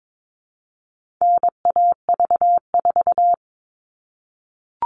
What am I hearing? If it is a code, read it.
Morse code, heard as a keyed tone, about 22 wpm, 700 Hz. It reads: NAV4
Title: All Cut Numbers - 4-Digit 22wpm